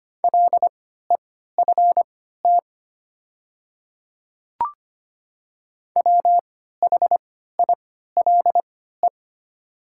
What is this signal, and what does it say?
Morse code, heard as a keyed tone, about 25 wpm, spaced as for 14 wpm, 700 Hz. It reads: LEFT WHILE
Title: 3x Character Spacing - Top 300 Words - 25wpm